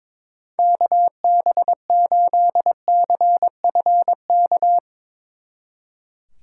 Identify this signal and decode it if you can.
Morse code, heard as a keyed tone, about 24 wpm, 700 Hz. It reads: KB8CFK
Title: Speed Racing - USA 2x3 Call Signs 22wpm